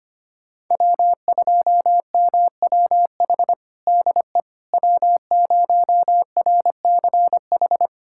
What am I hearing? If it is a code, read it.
Morse code, heard as a keyed tone, about 25 wpm, 700 Hz. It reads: W2MWH DE W0RCH
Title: USA Callsign DE Callsign - 5 Characters - 25wpm